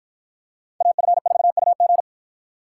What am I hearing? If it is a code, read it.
Morse code, heard as a keyed tone, about 53 wpm, 700 Hz. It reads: AF4UG